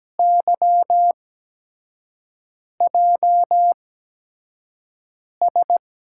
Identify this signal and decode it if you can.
Morse code, heard as a keyed tone, about 17 wpm, spaced as for 5 wpm, 700 Hz. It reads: YJS
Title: USA Call Signs - 6 Characters - 8x Character Spacing - 17wpm